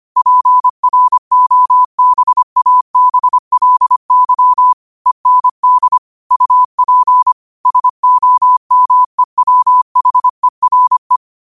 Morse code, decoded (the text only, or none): PROBABLY END UP SOMEWHERE